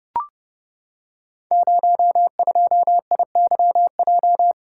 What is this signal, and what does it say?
Morse code, heard as a keyed tone, about 28 wpm, 700 Hz. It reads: 02IYJ
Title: Code Groups - 5 Letter:Numbers - 30wpm